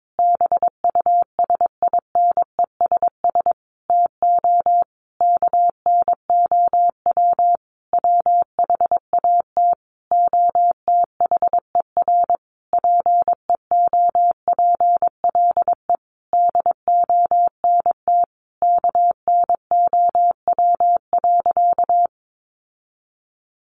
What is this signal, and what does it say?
Morse code, heard as a keyed tone, about 22 wpm, 700 Hz. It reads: BUSINESS TO KNOW WHAT OTHER PEOPLE DONT KNOW.
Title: Sentences from Top 600 Words 22wpm